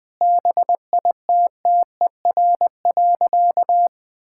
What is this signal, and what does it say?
Morse code, heard as a keyed tone, about 20 wpm, 700 Hz. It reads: BITTER.
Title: All-Age Appropriate Jokes - 20wpm